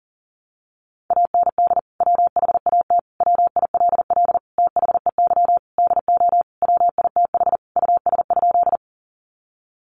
Morse code, heard as a keyed tone, about 40 wpm, 700 Hz. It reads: AND WHAT WILL THEY DO WITH US?
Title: Sentences from Top 200 Words 40wpm